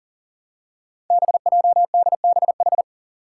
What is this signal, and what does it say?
Morse code, heard as a keyed tone, about 40 wpm, 700 Hz. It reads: BJDBH
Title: Five Challenging Letter-Number Combinations 40wpm